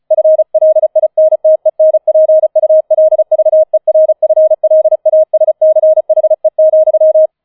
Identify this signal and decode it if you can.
Morse code, heard as a keyed tone, about 36 wpm, 630 Hz. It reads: FLINTENPULVERFLASCHE,
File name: 004 Vierter Streich 35wpm